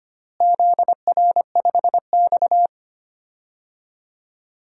Morse code, heard as a keyed tone, about 25 wpm, 700 Hz. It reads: ZR5X
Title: DXCC-100 Call Signs - 4 Characters - Repeated 3x - 25wpm